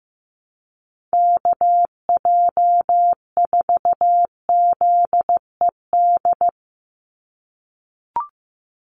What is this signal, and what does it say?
Morse code, heard as a keyed tone, about 15 wpm, 700 Hz. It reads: KJ4ZED
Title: USA Call Signs 15wpm